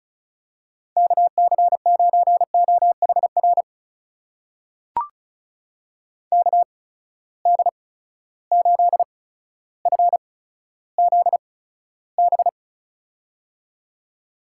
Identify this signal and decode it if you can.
Morse code, heard as a keyed tone, about 36 wpm, spaced as for 15 wpm, 700 Hz. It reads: KC9OHR KD8FZB
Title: USA Call Signs - 6 Characters - 8x Character Spacing - 35wpm